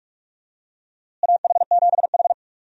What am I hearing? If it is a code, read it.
Morse code, heard as a keyed tone, about 42 wpm, 700 Hz. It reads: AH7H
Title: USA Call Signs - 4 Characters - Repeated 3x - 45wpm